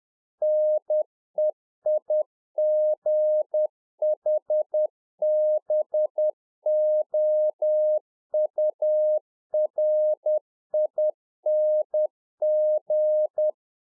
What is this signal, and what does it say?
Morse code, heard as a keyed tone, about 10 wpm, 610 Hz. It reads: NEIGHBOURING
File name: 2011-04-mp3 / RICK JJ0MPI 10WPM